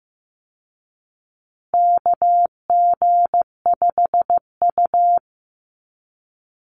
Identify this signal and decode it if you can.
Morse code, heard as a keyed tone, about 14 wpm, 700 Hz. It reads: KG5U
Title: Call Signs Repeated 3x - 15wpm